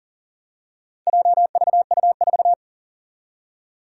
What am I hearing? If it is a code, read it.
Morse code, heard as a keyed tone, about 42 wpm, 700 Hz. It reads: JVU4
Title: Four Challenging Letter-Number Combinations 40wpm